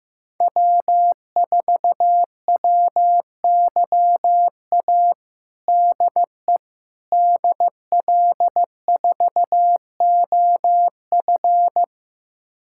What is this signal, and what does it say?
Morse code, heard as a keyed tone, about 15 wpm, 700 Hz. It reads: W4WYA DE DL4OF
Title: USA Callsign DE Callsign - 5 Characters - 15wpm